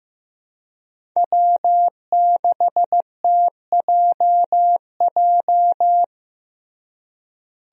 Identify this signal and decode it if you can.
Morse code, heard as a keyed tone, about 14 wpm, 700 Hz. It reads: W6TJJ
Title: USA Call Signs - 5 Characters - Repeated 3x - 15wpm